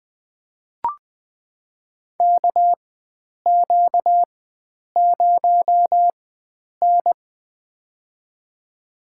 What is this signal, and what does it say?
Morse code, heard as a keyed tone, about 20 wpm, spaced as for 9 wpm, 700 Hz. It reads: KQ0N
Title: USA Call Signs - 4 Characters - 4x Character Spacing - 20wpm